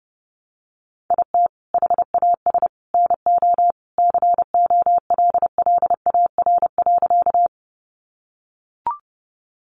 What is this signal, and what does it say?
Morse code, heard as a keyed tone, about 30 wpm, 700 Hz. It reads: IT HAS NO COLLAR.
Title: Easy Phrases 30wpm